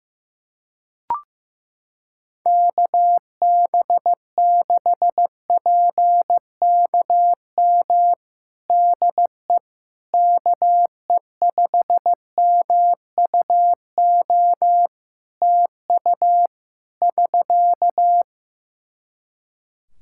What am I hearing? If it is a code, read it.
Morse code, heard as a keyed tone, about 15 wpm, 700 Hz. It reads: KB6PKM DE KE5MUO TU <SK>
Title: Practice QSO - Exchange 4 - Ending - 15wpm